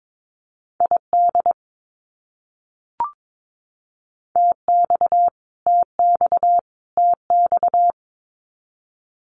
Morse code, heard as a keyed tone, about 22 wpm, 700 Hz. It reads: ID TX TX TX
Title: US States Repeated 3x 22wpm